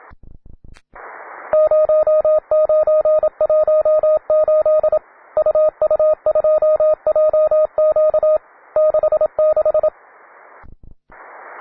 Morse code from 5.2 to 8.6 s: UU2JQ